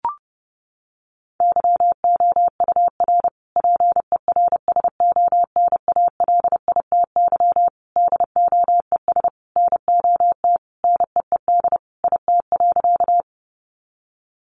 Morse code, heard as a keyed tone, about 30 wpm, 700 Hz. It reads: YOUR PERSONALITY DOES NOT NEED IT.
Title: Sentences from Top 1400 Words 30wpm